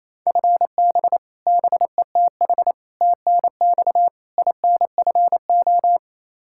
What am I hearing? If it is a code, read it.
Morse code, heard as a keyed tone, about 28 wpm, 700 Hz. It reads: FB BETH TNX INFO
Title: Practice QSO - Exchange 3 - More Info - 28wpm